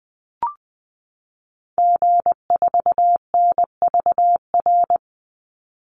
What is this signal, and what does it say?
Morse code, heard as a keyed tone, about 20 wpm, 700 Hz. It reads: G4NVR
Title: Call Signs Repeated 3x - 20wpm